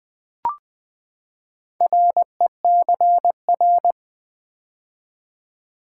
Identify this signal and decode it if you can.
Morse code, heard as a keyed tone, about 20 wpm, 700 Hz. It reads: RECR
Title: Build A Word - 4 Letters At A Time - 11-Letter Words - 20wpm